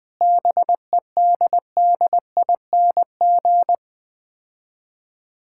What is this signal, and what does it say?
Morse code, heard as a keyed tone, about 20 wpm, 690 Hz. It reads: BEDDING
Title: Difficult Words with Hints - 20wpm